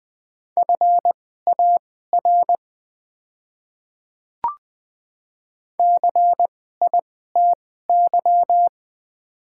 Morse code, heard as a keyed tone, about 20 wpm, spaced as for 14 wpm, 690 Hz. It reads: FAR CITY
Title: 2x Character Spacing - Top 300 Words - 20wpm